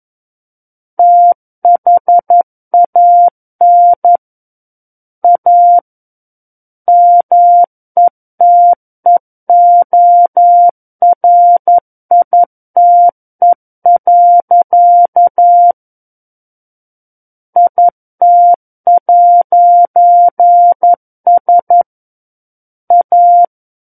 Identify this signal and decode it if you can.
Morse code, heard as a keyed tone, about 11 wpm, 700 Hz. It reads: THAN A METEORITE. IT'S A
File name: Book One / 02-Chapter 2 - 11 WPM